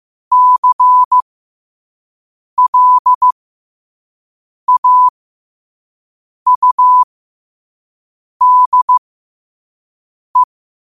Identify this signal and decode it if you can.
Morse code, heard as a keyed tone, about 15 wpm, spaced as for 5 wpm, 1000 Hz. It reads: CLAUDE
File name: qotdfiles / 5wpm